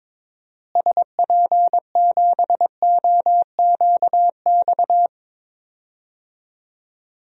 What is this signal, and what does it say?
Morse code, heard as a keyed tone, about 21 wpm, 700 Hz. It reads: SP7OQX
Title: DXCC-100 Call Signs - 6 Characters - Repeated 2x - 22wpm